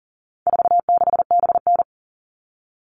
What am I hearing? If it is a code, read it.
Morse code, heard as a keyed tone, about 40 wpm, 700 Hz. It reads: 46BN
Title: All Cut Numbers - 4-Digit 40wpm